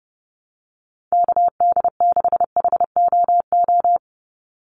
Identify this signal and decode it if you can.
Morse code, heard as a keyed tone, about 30 wpm, 700 Hz. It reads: KD6HOO
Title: USA Call Signs 30wpm